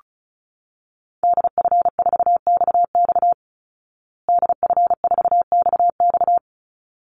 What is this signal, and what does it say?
Morse code, heard as a keyed tone, about 34 wpm, 700 Hz. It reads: DF4XX DF4XX
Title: Call Signs Repeated 3x - 35wpm